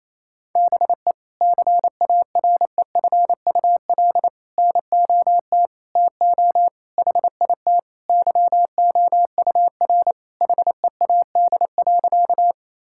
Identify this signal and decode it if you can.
Morse code, heard as a keyed tone, about 28 wpm, 700 Hz. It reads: BE CAREFUL NOT TO HIT YOUR HEAD.
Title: Sentences from Top 1600 Words 28wpm